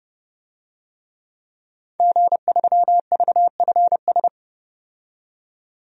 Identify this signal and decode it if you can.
Morse code, heard as a keyed tone, about 29 wpm, 700 Hz. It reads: G3VFS
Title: DXCC-100 Call Signs - 5 Characters - 30wpm